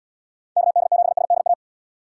Morse code, heard as a keyed tone, about 78 wpm, 700 Hz. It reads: KD7SDA